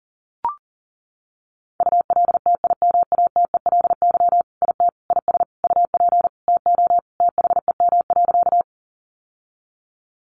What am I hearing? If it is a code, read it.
Morse code, heard as a keyed tone, about 40 wpm, 700 Hz. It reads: ULTIMATELY IT IS UP TO THEM.
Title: Sentences from Top 1200 Words 40wpm